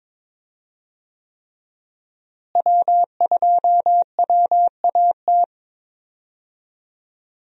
Morse code, heard as a keyed tone, about 22 wpm, 700 Hz. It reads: W2WAT
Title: USA Call Signs - 5 Characters - Repeated 3x - 22wpm